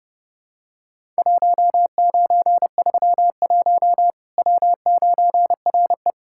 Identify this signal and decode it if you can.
Morse code, heard as a keyed tone, about 30 wpm, 700 Hz. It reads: 1931 W9RE
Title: Worked All Europe DX Contest - QTC Exchange - 30wpm